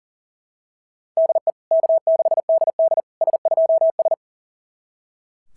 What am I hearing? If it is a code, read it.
Morse code, heard as a keyed tone, about 40 wpm, 690 Hz. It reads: DE K6DD S2S
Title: SOTA QSO Practice 40wpm